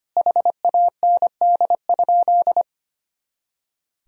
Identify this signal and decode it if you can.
Morse code, heard as a keyed tone, about 25 wpm, 700 Hz. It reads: HAND?
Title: Sentences from Top 200 Words and Abbreviations 25wpm